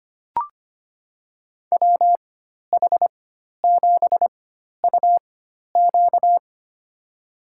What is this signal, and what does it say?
Morse code, heard as a keyed tone, about 25 wpm, spaced as for 12 wpm, 700 Hz. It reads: WH7UQ
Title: USA Call Signs - 5 Characters - 4x Character Spacing - 25wpm